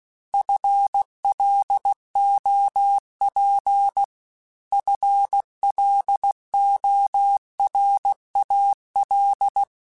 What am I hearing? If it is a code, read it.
Morse code, heard as a keyed tone, about 16 wpm, 800 Hz. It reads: FLOP FLORAL